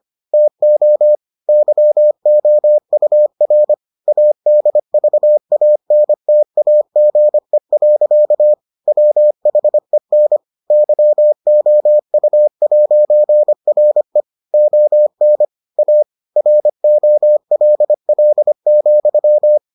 Morse code, heard as a keyed tone, about 25 wpm, 600 Hz. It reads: TO YOUR ADVANTAGE. WHEN YOU'RE ON A ROLL,